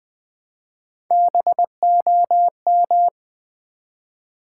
Morse code, heard as a keyed tone, about 20 wpm, 700 Hz. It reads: BOM
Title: Build A Word - 1 Letter At A Time - 3 to 5-Letter Words - 20wpm